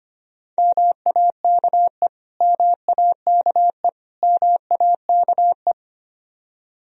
Morse code, heard as a keyed tone, about 25 wpm, 700 Hz. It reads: MAKE MAKE MAKE
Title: Top 100 Words Repeated 25wpm